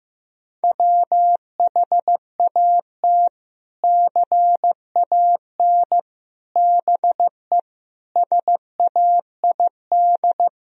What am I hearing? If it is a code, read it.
Morse code, heard as a keyed tone, about 15 wpm, 700 Hz. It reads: WHAT CAN BE SAID
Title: Sentences from Top 1000 Words - Review Part 4 - 15wpm